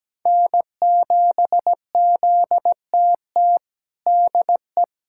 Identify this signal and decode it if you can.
Morse code, heard as a keyed tone, about 17 wpm, 700 Hz. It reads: N7ZTT DE
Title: USA Callsign DE Callsign - 5 Characters - 17wpm